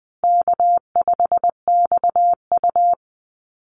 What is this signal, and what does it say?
Morse code, heard as a keyed tone, about 20 wpm, 700 Hz. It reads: K5XU
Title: Easy Call Signs Repeated 3x - 20wpm